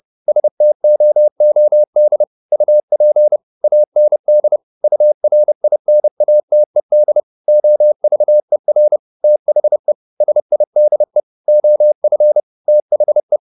Morse code, heard as a keyed tone, about 30 wpm, 600 Hz. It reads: STOOD UP AND URINATED OVER THE SIDE OF THE
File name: the old man and the sea / the old man and the sea 1 30wpm